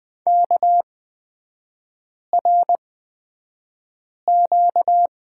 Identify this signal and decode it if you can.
Morse code, heard as a keyed tone, about 20 wpm, spaced as for 5 wpm, 700 Hz. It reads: KRQ
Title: 236 Lesson 38e - Letter q - Three Characters 20wpm